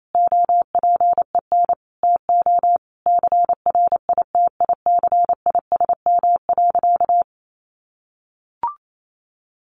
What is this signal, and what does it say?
Morse code, heard as a keyed tone, about 28 wpm, 700 Hz. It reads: OPEN TO CRITICISM.
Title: Sentences from Top 1300 Words 28wpm